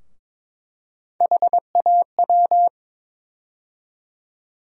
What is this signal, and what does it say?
Morse code, heard as a keyed tone, about 22 wpm, 700 Hz. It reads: HAW
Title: USA Call Sign Postfixes - 3 Letters 22wpm